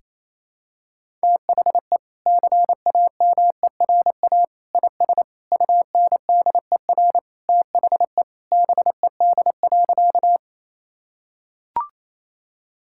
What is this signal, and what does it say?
Morse code, heard as a keyed tone, about 28 wpm, 700 Hz. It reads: THE CAMERA IS UNDER THE BED.
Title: Sentences from Top 1000 Words - Review Part 5 - 28wpm